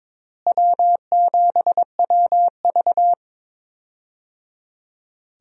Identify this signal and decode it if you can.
Morse code, heard as a keyed tone, about 22 wpm, 700 Hz. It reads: W7WV
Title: USA Call Signs - 4 Characters - Repeated 3x - 22wpm